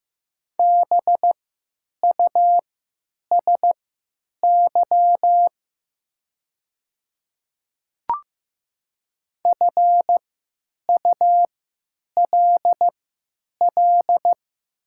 Morse code, heard as a keyed tone, about 15 wpm, spaced as for 9 wpm, 700 Hz. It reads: BUSY FULL
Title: 3x Character Spacing - Top 500 Words - 15wpm